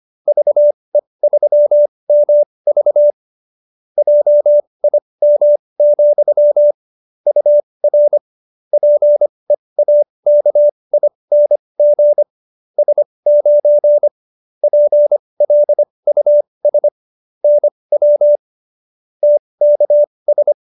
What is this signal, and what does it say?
Morse code, heard as a keyed tone, about 25 wpm, spaced as for 20 wpm, 590 Hz. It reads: VE3MV JIM, UR PEAKING S9 PLUS NW TKS